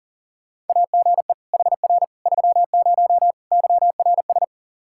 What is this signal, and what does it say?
Morse code, heard as a keyed tone, about 37 wpm, 700 Hz. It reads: AGE HR 30 YRS
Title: Practice QSO - Exchange 3 - More Info - 40wpm